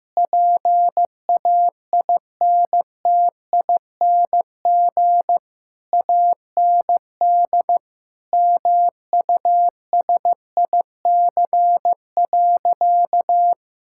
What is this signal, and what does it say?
Morse code, heard as a keyed tone, about 15 wpm, 700 Hz. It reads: PAINTING AND MUSIC.
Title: Sentences from Top 1000 Words - Review Part 6 - 15wpm